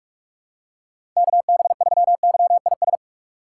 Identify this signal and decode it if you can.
Morse code, heard as a keyed tone, about 46 wpm, 700 Hz. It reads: KB3YIS
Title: USA Call Signs - 6 Characters - 4x Character Spacing - 45wpm